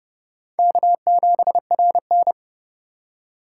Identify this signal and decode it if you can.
Morse code, heard as a keyed tone, about 35 wpm, 700 Hz. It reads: K7RN
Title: Speed Racing - USA 1x2 Call Signs 30wpm